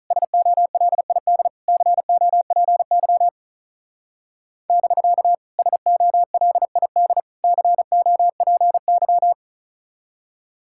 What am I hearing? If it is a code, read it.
Morse code, heard as a keyed tone, about 34 wpm, 700 Hz. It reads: SOLID COPY <BK> SOLID COPY
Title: Speed-Racing Common QSO Phrases - 35wpm